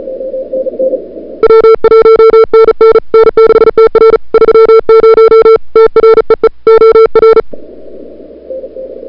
Morse code, heard as a keyed tone, about 36 wpm, 440 Hz. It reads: W1NN N6TR 30 TREE OR
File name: CWSprint / February 2007 CW Sprint / 01-2007 CW Sprint Hour1